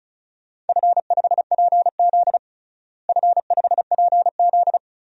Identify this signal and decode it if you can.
Morse code, heard as a keyed tone, about 34 wpm, 700 Hz. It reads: F5PZ F5PZ
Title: DXCC-100 Call Signs - 4 Characters - Repeated 3x - 35wpm